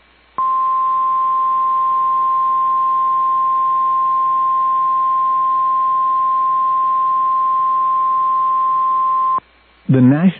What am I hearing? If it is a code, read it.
Emergency Alert System (SAME) tones (the 1050 Hz attention tone). It is heard from 0.6 to 9.3 s.